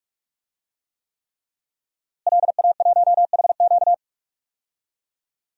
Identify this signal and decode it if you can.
Morse code, heard as a keyed tone, about 41 wpm, 700 Hz. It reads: LA1HQ
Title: DXCC-100 Call Signs - 5 Characters - Repeated 3x - 45wpm